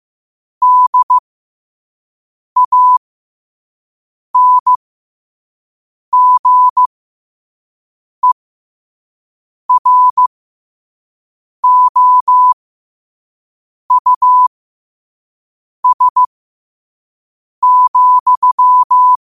Morse code, heard as a keyed tone, about 15 wpm, spaced as for 5 wpm, 1000 Hz. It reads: DANGEROUS,
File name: qotdfiles / 5wpm